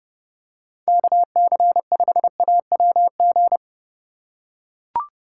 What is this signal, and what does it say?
Morse code, heard as a keyed tone, about 30 wpm, 700 Hz. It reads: KC5AWG